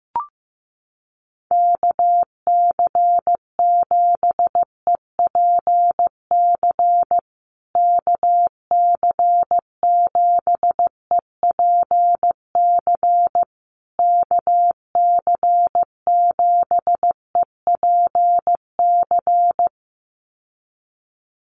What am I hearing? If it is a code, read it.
Morse code, heard as a keyed tone, about 15 wpm, 700 Hz. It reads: KC7EPC KC7EPC KC7EPC
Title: USA Call Signs Repeated 3x 15wpm